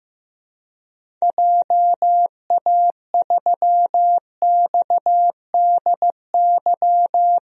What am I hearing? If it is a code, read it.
Morse code, heard as a keyed tone, about 15 wpm, 700 Hz. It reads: JA3XDY
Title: DXCC-100 Call Signs - 6 Characters - Repeated 2x - 15wpm